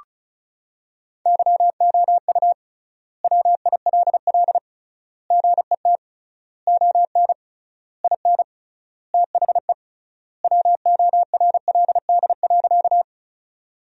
Morse code, heard as a keyed tone, about 36 wpm, 700 Hz. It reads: YOU WILL GET ON IN THE WORLD.
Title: 3x Word-Spacing - Sentences from Top 200 Words - 35wpm